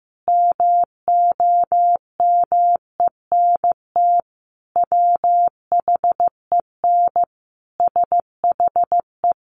Morse code, heard as a keyed tone, about 15 wpm, 700 Hz. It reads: MOMENT WHEN SHE
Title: Sentences from Top 1000 Words 15wpm